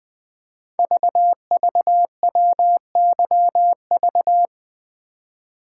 Morse code, heard as a keyed tone, about 20 wpm, 700 Hz. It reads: VVWYV
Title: Code Groups - 5 Letters - 20wpm